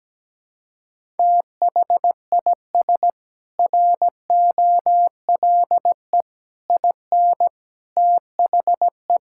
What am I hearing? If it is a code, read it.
Morse code, heard as a keyed tone, about 17 wpm, 700 Hz. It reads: THIS ROLE IN THE
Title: Sentences from Top 1000 Words - Review Part 6 - 17wpm